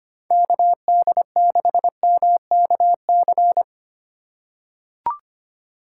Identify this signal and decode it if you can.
Morse code, heard as a keyed tone, about 25 wpm, 700 Hz. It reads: KD6MKC
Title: USA Call Signs - 6 Characters - 2x Character Spacing - 25wpm